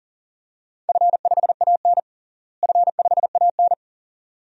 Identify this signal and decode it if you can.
Morse code, heard as a keyed tone, about 39 wpm, 700 Hz. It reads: F5AN F5AN
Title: DXCC-100 Call Signs - 4 Characters - Repeated 2x - 40wpm